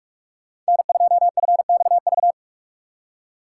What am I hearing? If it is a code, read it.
Morse code, heard as a keyed tone, about 54 wpm, 700 Hz. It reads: N2FXV
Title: Speed Racing - USA 1x3 Call Signs 45wpm